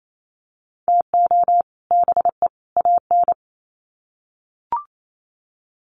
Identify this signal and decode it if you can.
Morse code, heard as a keyed tone, about 28 wpm, 700 Hz. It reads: TO BE AN
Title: Sets of 3 Words 28wpm